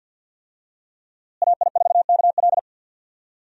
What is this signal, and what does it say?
Morse code, heard as a keyed tone, about 49 wpm, 700 Hz. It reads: AI4KL
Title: USA Call Signs - 5 Characters - Repeated 2x - 50wpm